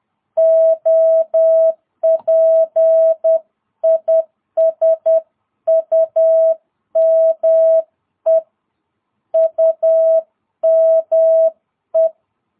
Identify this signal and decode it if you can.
Morse code, heard as a keyed tone, about 10 wpm, 660 Hz. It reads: OP IS UME UME
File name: jh8cbh / 2022cw